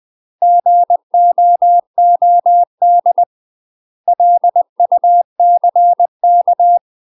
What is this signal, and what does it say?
Morse code, heard as a keyed tone, about 20 wpm, 700 Hz. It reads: GOOD LUCK
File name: UK MASTER QSOs TRAINING / 20WPM / UK MASTER QSOs 0052